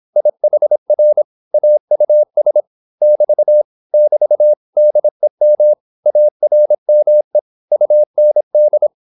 Morse code, heard as a keyed tone, about 26 wpm, 600 Hz. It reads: IHR AUS <BT> <BT> DEM ARME UND